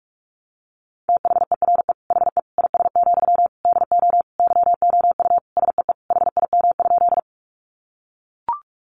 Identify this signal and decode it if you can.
Morse code, heard as a keyed tone, about 44 wpm, 700 Hz. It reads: THERE HE IS, DO YOU SEE HIM?
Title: Sentences from Top 100 Words 45wpm